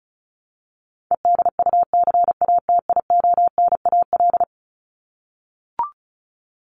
Morse code, heard as a keyed tone, about 34 wpm, 700 Hz. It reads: EDUCATIONAL
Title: Top 1001-1250 Words 35wpm